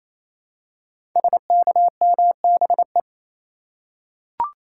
Morse code, heard as a keyed tone, about 27 wpm, 700 Hz. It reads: SKMBE
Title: Code Groups - 5 Letters - 28wpm